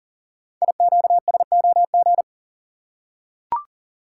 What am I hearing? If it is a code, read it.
Morse code, heard as a keyed tone, about 38 wpm, 700 Hz. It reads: IQSOG
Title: Code Groups - 5 Letters - 40wpm